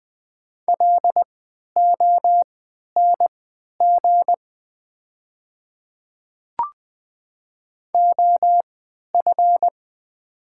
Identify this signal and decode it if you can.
Morse code, heard as a keyed tone, about 20 wpm, spaced as for 11 wpm, 700 Hz. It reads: LONG OF
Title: 3x Character Spacing - Top 100 Words - 20wpm